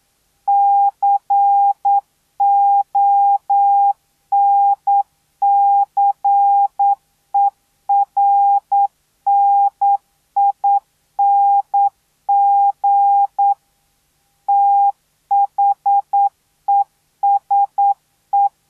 Morse code, heard as a keyed tone, about 9 wpm, 800 Hz. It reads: CONCERNING THESE